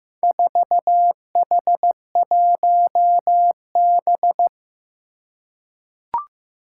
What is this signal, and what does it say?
Morse code, heard as a keyed tone, about 15 wpm, 700 Hz. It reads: 4H1B
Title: Four Challenging Letter-Number Combinations 15wpm